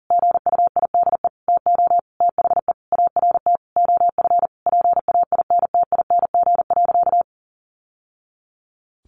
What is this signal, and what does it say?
Morse code, heard as a keyed tone, about 40 wpm, 700 Hz. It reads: GUIDE TO THE ART OF PAINTING.
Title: Sentences from Top 1200 Words 40wpm